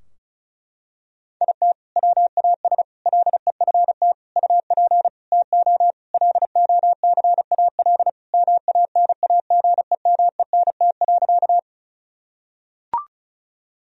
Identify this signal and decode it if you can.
Morse code, heard as a keyed tone, about 34 wpm, 700 Hz. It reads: IT WAS LEFT UP TO LOCAL MANAGEMENT.
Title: Sentences from Top 1000 Words - Review Part 4 - 35wpm